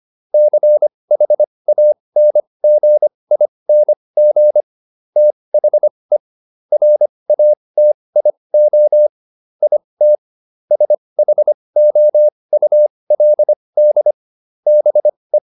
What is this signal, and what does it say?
Morse code, heard as a keyed tone, about 25 wpm, spaced as for 20 wpm, 600 Hz. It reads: CHANGING THE RATIO IT SHOULD BE
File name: SS316 20